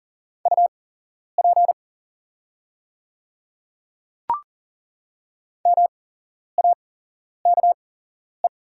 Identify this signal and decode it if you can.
Morse code, heard as a keyed tone, about 40 wpm, spaced as for 11 wpm, 690 Hz. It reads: UP MAKE
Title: 8x Character Spacing - Top 100 Words - 40wpm